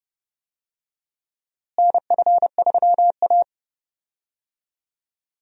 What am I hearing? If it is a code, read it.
Morse code, heard as a keyed tone, about 32 wpm, 700 Hz. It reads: NF3A
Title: USA Call Signs - 4 Characters - Repeated 2x - 30wpm